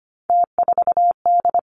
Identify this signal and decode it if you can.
Morse code, heard as a keyed tone, about 25 wpm, 700 Hz. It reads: T4D
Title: Three Challenging Letter-Number Combinations 25wpm